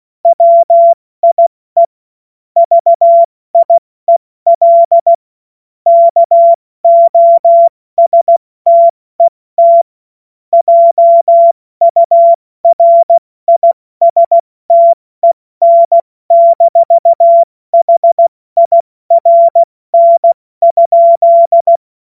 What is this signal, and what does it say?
Morse code, heard as a keyed tone, about 16 wpm, 680 Hz. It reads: WIE VIEL KOSTET JURISTEN-HIRN?
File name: Morsekurs / Witze / Witz 07. Juni 2022 70 F80